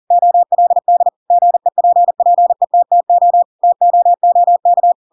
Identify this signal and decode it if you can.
Morse code, heard as a keyed tone, about 40 wpm, 700 Hz. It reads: OLD GEPPETTO TOOK
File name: Pinocchio / Pinocchio ch7 40 40 ik1hge